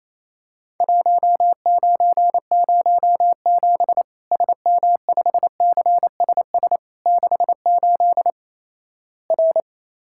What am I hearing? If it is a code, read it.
Morse code, heard as a keyed tone, about 28 wpm, 700 Hz. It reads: 1907 SM5CSS 68 R
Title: Worked All Europe DX Contest - QTC Exchange - 28wpm